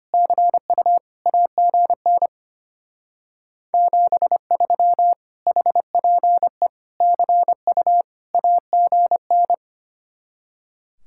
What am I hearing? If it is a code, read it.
Morse code, heard as a keyed tone, about 32 wpm, 700 Hz. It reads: CU AGN 73 HPE CU AGN